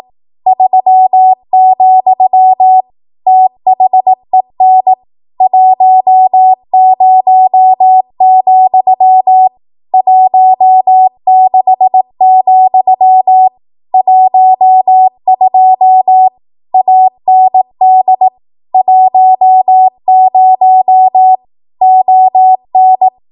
Morse code, heard as a keyed tone, about 18 wpm, 750 Hz. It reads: 3, THEN 10, 16, 12 AND 10 ON
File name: Archive / 18 WPM / 18WPM